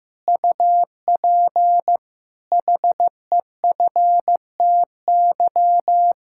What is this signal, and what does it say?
Morse code, heard as a keyed tone, about 15 wpm, 700 Hz. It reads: UP HEFTY
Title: Pangram Challenge v2 - 15wpm